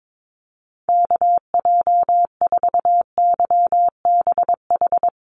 Morse code, heard as a keyed tone, about 22 wpm, 700 Hz. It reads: KJ4YBH
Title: USA Call Signs Repeated 2x 22wpm